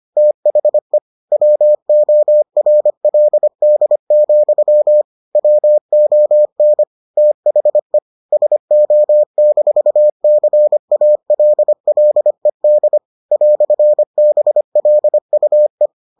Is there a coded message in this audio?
Yes: Morse code from 0.2 to 15.9 s, heard as a keyed tone, about 25 wpm, 600 Hz. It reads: THE WORLD, WON THE SO-CALLED "BLUE